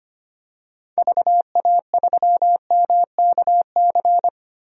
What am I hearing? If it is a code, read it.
Morse code, heard as a keyed tone, about 25 wpm, 700 Hz. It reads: VA3MKC
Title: DXCC-100 Call Signs - 6 Characters - Repeated 3x - 25wpm